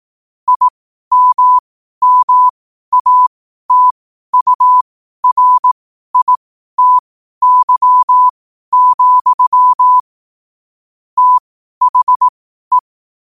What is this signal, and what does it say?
Morse code, heard as a keyed tone, about 18 wpm, spaced as for 12 wpm, 1000 Hz. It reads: IMMATURITY, THE
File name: qotdfiles / 12wpm